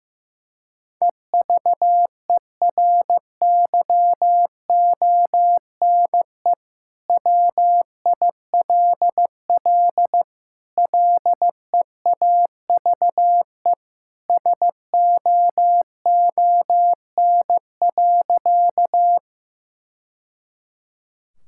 Morse code, heard as a keyed tone, about 15 wpm, 700 Hz. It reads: EVERYONE WILL LEAVE SOON.
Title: Sentences from Top 1000 Words - Review Part 5 - 15wpm